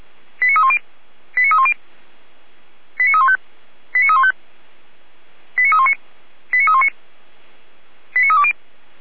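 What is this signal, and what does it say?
Five-tone selective call (ZVEI), heard as quick runs of five tones, tones of about 70 ms. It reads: 89310 89310 89315 89315 89319 89319 89320
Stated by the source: Received on SDR, from feuerwehr-orlamuende.org, the Orlamünde fire brigade.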